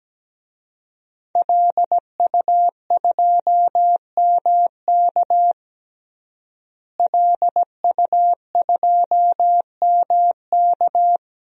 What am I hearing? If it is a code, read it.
Morse code, heard as a keyed tone, about 17 wpm, 700 Hz. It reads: LU2MK LU2MK
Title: DXCC-100 Call Signs - 5 Characters - Repeated 3x - 17wpm